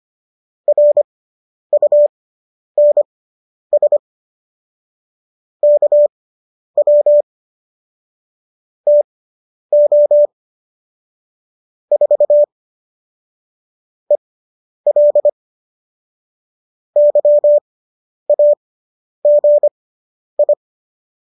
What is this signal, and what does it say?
Morse code, heard as a keyed tone, about 25 wpm, spaced as for 10 wpm, 590 Hz. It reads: RUNS KW TO 4 EL YAGI